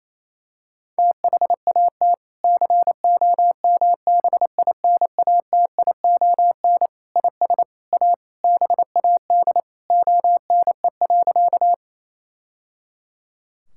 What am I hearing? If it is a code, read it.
Morse code, heard as a keyed tone, about 28 wpm, 700 Hz. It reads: THAT COMBINATION IS A BAD ONE.
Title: Sentences from Top 1000 Words - Review Part 6 - 28wpm